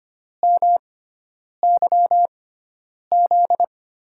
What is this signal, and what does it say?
Morse code, heard as a keyed tone, about 25 wpm, spaced as for 9 wpm, 700 Hz. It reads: MYZ
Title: USA Call Signs - 6 Characters - 6x Character Spacing - 25wpm